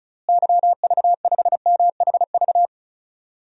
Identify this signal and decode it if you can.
Morse code, heard as a keyed tone, about 36 wpm, 700 Hz. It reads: YV5MHV
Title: DXCC-100 Call Signs - 6 Characters - Repeated 3x - 35wpm